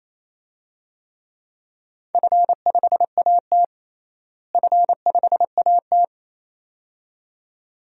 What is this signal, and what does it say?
Morse code, heard as a keyed tone, about 28 wpm, 700 Hz. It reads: F5AT F5AT
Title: DXCC-100 Call Signs - 4 Characters - Repeated 2x - 28wpm